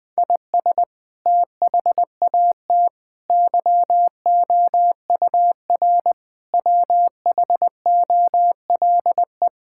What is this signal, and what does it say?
Morse code, heard as a keyed tone, about 21 wpm, 700 Hz. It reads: IS THAT YOUR WHOLE
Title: Sentences from Top 1000 Words - Review Part 5 - 20wpm